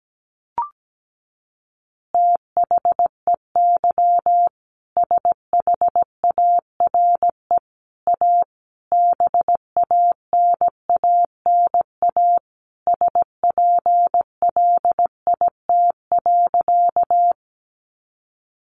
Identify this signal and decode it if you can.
Morse code, heard as a keyed tone, about 17 wpm, 700 Hz. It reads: THEY SHARE A BANANA SPLIT.
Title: Easy Phrases 17wpm